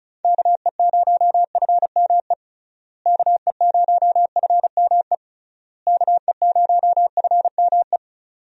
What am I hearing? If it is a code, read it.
Morse code, heard as a keyed tone, about 34 wpm, 700 Hz. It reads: KE0FME KE0FME KE0FME
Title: USA Call Signs - 6 Characters - Repeated 3x - 35wpm